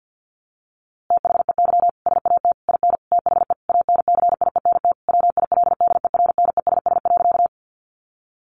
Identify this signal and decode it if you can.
Morse code, heard as a keyed tone, about 49 wpm, 700 Hz. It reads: THEY SAT IN THE ANCIENT WILDERNESS.
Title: A Sound Of Thunder 50wpm